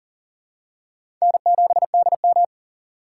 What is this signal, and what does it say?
Morse code, heard as a keyed tone, about 40 wpm, 700 Hz. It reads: N7DM
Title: USA Call Signs - 4 Characters - 4x Character Spacing - 40wpm